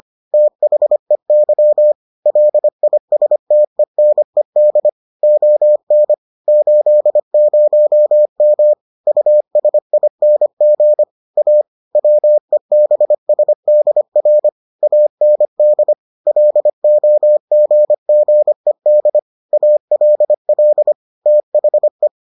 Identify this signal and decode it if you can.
Morse code, heard as a keyed tone, about 25 wpm, 600 Hz. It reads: THEY LISTENED ON 80M USING A WEBSDR AND LOGGED ALL THE